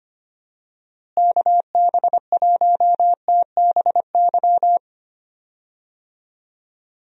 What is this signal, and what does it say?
Morse code, heard as a keyed tone, about 25 wpm, 700 Hz. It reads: KB1TBY
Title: USA Call Signs - 6 Characters - Repeated 3x - 25wpm